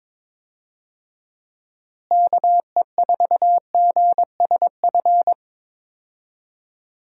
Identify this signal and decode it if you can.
Morse code, heard as a keyed tone, about 22 wpm, 700 Hz. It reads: KE4GSF
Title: USA Call Signs - 6 Characters - Repeated 2x - 22wpm